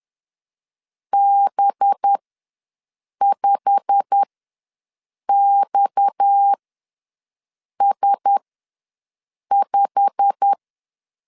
Morse code, heard as a keyed tone, about 10 wpm, spaced as for 6 wpm, 800 Hz. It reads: B5XS5
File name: LEZIONI / 8 YJ45 / 8 YJ45 ws 10wpm 3843